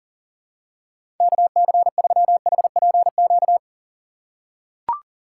Morse code, heard as a keyed tone, about 40 wpm, 700 Hz. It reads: KC3HPQ
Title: USA Call Signs - 6 Characters - 3x Character Spacing - 40wpm